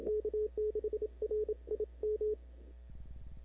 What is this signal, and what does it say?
Morse code, heard as a keyed tone, about 25 wpm, 440 Hz. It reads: K6RIM
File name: CW / 2005-k5tr arrl cw dx -03:00